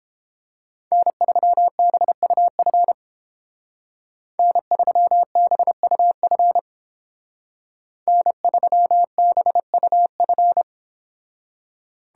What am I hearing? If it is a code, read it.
Morse code, heard as a keyed tone, about 32 wpm, 700 Hz. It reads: N3BUF N3BUF N3BUF